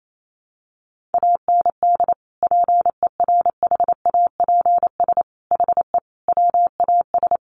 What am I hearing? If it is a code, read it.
Morse code, heard as a keyed tone, about 28 wpm, 700 Hz. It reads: AND PERHAPS HE WAS
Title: Sentences from Top 900 Words 28wpm